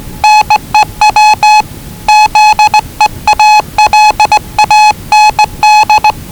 Morse code, heard as a keyed tone, about 20 wpm, 840 Hz. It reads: NEW ZEALAND
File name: WellingtonRadioZlwCloseDownOnHfAndMf / 01-Zlw 4277khz 417.5 qrt until 0925z